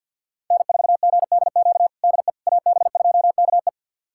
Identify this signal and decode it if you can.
Morse code, heard as a keyed tone, about 46 wpm, 700 Hz. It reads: N4GDQ DE AB2KE
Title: USA Callsign DE Callsign - 5 Characters - 50wpm